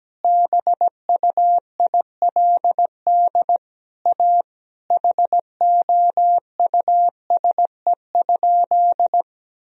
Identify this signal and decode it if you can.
Morse code, heard as a keyed tone, about 17 wpm, 700 Hz. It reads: BUILD A HOUSE?
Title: Sentences from Top 1000 Words - Review Part 1 - 17wpm